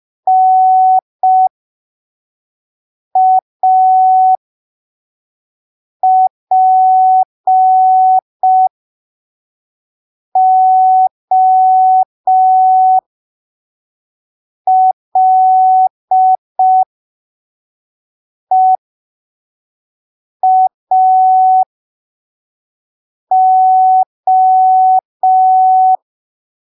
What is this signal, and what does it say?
Morse code, heard as a keyed tone, about 5 wpm, spaced as for 3 wpm, 750 Hz. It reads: NAPOLEAO
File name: arquivos / texto / texto 26 05ppm